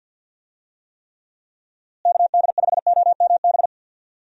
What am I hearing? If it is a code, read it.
Morse code, heard as a keyed tone, about 52 wpm, 700 Hz. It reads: KD5OMB
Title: USA Call Signs - 6 Characters - Repeated 3x - 50wpm